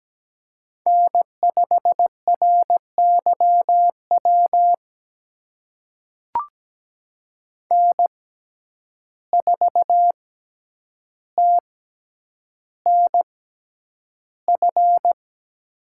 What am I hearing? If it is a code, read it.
Morse code, heard as a keyed tone, about 17 wpm, spaced as for 9 wpm, 700 Hz. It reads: N5RYW N4TNF
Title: USA Call Signs - 5 Characters - 6x Character Spacing - 17wpm